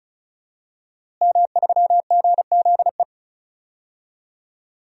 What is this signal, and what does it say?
Morse code, heard as a keyed tone, about 34 wpm, 700 Hz. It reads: M3GZE
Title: DXCC-100 Call Signs - 5 Characters - Repeated 2x - 35wpm